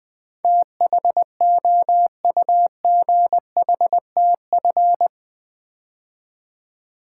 Morse code, heard as a keyed tone, about 20 wpm, 700 Hz. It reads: THOUGHTF
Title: Build A Word - 4 Letters At A Time - 10-Letter Words - 20wpm